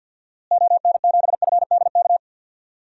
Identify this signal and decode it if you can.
Morse code, heard as a keyed tone, about 51 wpm, 700 Hz. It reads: ON7FDK
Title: DXCC-100 Call Signs - 6 Characters - Repeated 3x - 50wpm